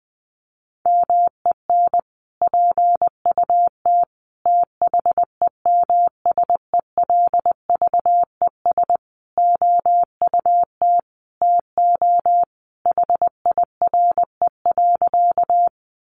Morse code, heard as a keyed tone, about 20 wpm, 700 Hz. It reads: MEN PUT THEMSELVES OUT TO HIRE.
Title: Sentences from Top 1400 Words 20wpm